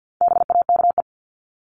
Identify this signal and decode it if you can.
Morse code, heard as a keyed tone, about 50 wpm, 700 Hz. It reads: BAKE
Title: Top 1751-2000 Words 50wpm